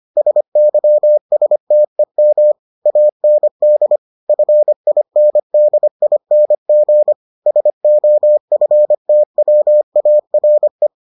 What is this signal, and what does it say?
Morse code, heard as a keyed tone, about 25 wpm, 600 Hz. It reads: SYSTEM AND FINDING SOFTWARE